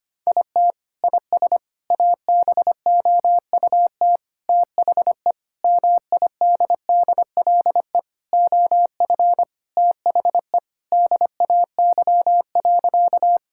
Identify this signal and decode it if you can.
Morse code, heard as a keyed tone, about 25 wpm, 700 Hz. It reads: IT IS ABOUT THE MIDDLE OF THE DAY.
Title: Sentences from Top 1000 Words - Review Part 6 - 25wpm